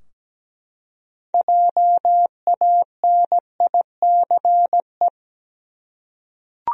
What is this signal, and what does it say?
Morse code, heard as a keyed tone, about 17 wpm, 700 Hz. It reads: JANICE
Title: USA Common Names - 17wpm